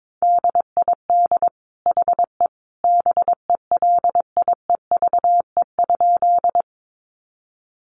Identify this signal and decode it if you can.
Morse code, heard as a keyed tone, about 22 wpm, 700 Hz. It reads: DID HE BELIEVE?
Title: Sentences from Top 1000 Words 22wpm